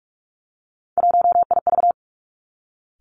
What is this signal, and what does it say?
Morse code, heard as a keyed tone, about 46 wpm, 700 Hz. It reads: 1IV
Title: Three Challenging Letter-Number Combinations 45wpm